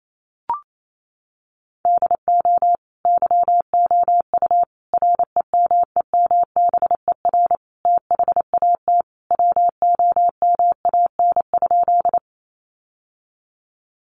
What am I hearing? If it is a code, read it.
Morse code, heard as a keyed tone, about 28 wpm, 700 Hz. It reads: DO YOU REMEMBER THAT WOMAN?
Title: Sentences from Top 1100 Words 28wpm